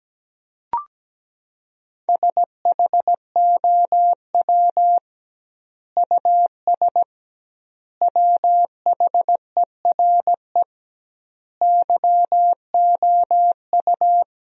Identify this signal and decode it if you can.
Morse code, heard as a keyed tone, about 17 wpm, 700 Hz. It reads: SHOW US WHERE YOU
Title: 2x Word-Spacing - Sentences from Top 200 Words - 17wpm